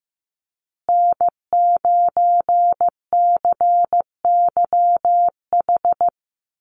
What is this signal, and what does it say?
Morse code, heard as a keyed tone, about 15 wpm, 700 Hz. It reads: N9CYH
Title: USA Call Signs Repeated 3x 15wpm